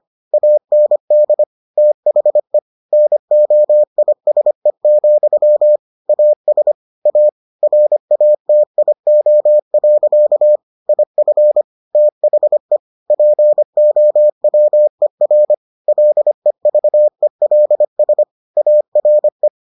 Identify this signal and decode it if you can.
Morse code, heard as a keyed tone, about 25 wpm, 600 Hz. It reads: AND THE NOISE, AS A RATIO. IF THE POWER LEVELS ARE